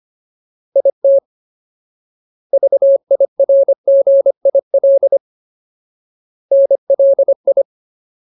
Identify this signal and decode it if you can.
Morse code, heard as a keyed tone, about 25 wpm, 550 Hz. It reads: IT VIRGIL NLI
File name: Ultimate Word List / 25WPM Ultimate